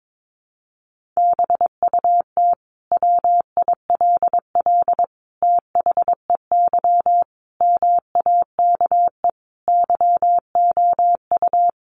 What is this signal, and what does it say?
Morse code, heard as a keyed tone, about 22 wpm, 700 Hz. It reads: BUT WILL THEY MAKE YOU
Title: Sentences from Top 1000 Words 22wpm